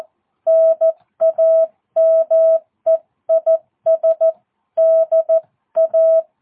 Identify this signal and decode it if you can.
Morse code, heard as a keyed tone, about 14 wpm, 650 Hz. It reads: NAME IS DA
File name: jh8cbh / 2022cw